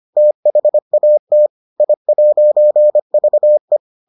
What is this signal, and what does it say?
Morse code, heard as a keyed tone, about 25 wpm, 600 Hz. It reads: THAT I'VE